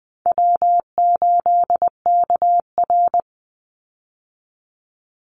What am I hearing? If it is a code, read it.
Morse code, heard as a keyed tone, about 20 wpm, 700 Hz. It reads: W8KR
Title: Easy Call Signs - 20wpm